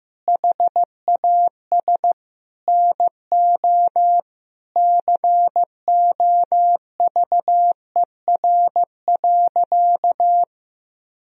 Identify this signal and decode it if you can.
Morse code, heard as a keyed tone, about 16 wpm, 700 Hz. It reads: HAS NO COVER.
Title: Sentences from Top 1000 Words - Review Part 1 - 15wpm